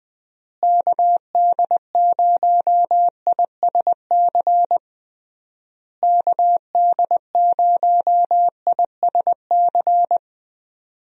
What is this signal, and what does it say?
Morse code, heard as a keyed tone, about 20 wpm, 700 Hz. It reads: KD0ISC KD0ISC
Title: USA Call Signs - 6 Characters - Repeated 3x - 20wpm